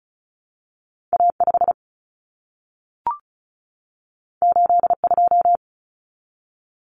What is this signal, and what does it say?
Morse code, heard as a keyed tone, about 36 wpm, 700 Hz. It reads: A5 82
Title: Common Cut Numbers - 2-Digit 35wpm